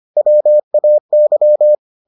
Morse code, heard as a keyed tone, about 25 wpm, 600 Hz. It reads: WAY